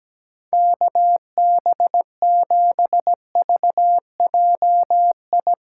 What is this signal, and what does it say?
Morse code, heard as a keyed tone, about 17 wpm, 700 Hz. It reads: KB7VJI
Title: USA Callsign DE Callsign - 6 Characters - 17wpm